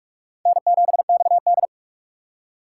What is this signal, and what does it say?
Morse code, heard as a keyed tone, about 60 wpm, spaced as for 52 wpm, 700 Hz. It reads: N7XD
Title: Speed Racing - USA 1x2 Call Signs 45wpm